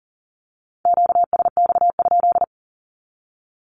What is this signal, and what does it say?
Morse code, heard as a keyed tone, about 39 wpm, 700 Hz. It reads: QSX?
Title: Q-Codes 40wpm